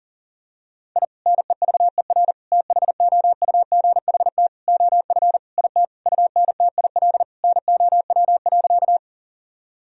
Morse code, heard as a keyed tone, about 40 wpm, 700 Hz. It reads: I NEVER THOUGHT OF IT UNTIL NOW.
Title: Sentences from Top 1000 Words - Review Part 1 - 40wpm